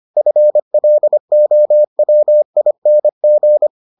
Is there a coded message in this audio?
Yes: Morse code from 0.2 to 3.7 s, heard as a keyed tone, about 25 wpm, 600 Hz. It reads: FLOWING